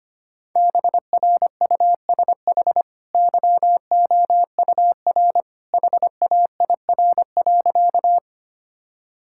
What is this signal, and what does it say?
Morse code, heard as a keyed tone, about 25 wpm, 690 Hz. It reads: BRUSH YOUR HAIR.
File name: Easy Sentences with Hints - 25wpm